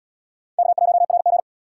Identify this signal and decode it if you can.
Morse code, heard as a keyed tone, about 66 wpm, 700 Hz. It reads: K1RG